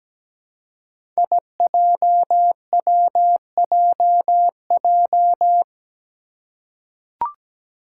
Morse code, heard as a keyed tone, about 17 wpm, 700 Hz. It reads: IJWJJ
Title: Five Challenging Letter-Number Combinations 17wpm